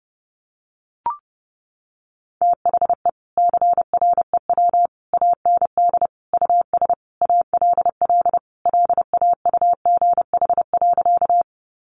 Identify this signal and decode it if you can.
Morse code, heard as a keyed tone, about 30 wpm, 700 Hz. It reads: THE CREW AND US ALL LAUGH.
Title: Sentences from Top 1500 Words 30wpm